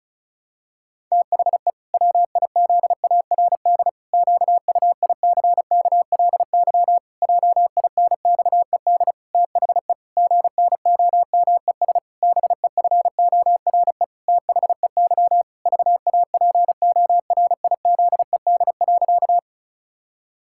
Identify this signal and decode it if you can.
Morse code, heard as a keyed tone, about 34 wpm, 700 Hz. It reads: THE WIZARD QUICKLY JINXED THE GNOMES BEFORE THEY VAPORIZED.
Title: Pangram Challenge v2 - 35wpm